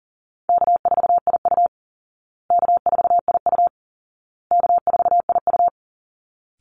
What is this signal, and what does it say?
Morse code, heard as a keyed tone, about 40 wpm, 700 Hz. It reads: K4IU K4IU K4IU
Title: Easy Call Signs Repeated 3x - 40wpm